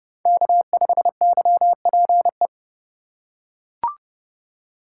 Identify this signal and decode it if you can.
Morse code, heard as a keyed tone, about 28 wpm, 700 Hz. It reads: K5YPE